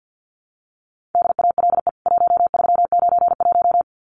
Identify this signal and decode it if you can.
Morse code, heard as a keyed tone, about 50 wpm, 700 Hz. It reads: DALE 1391
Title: CWT Exchange 50wpm